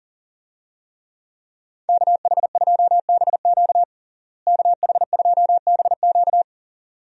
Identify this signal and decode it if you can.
Morse code, heard as a keyed tone, about 40 wpm, 700 Hz. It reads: KH2BQ KH2BQ
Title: USA Call Signs - 5 Characters - Repeated 2x - 40wpm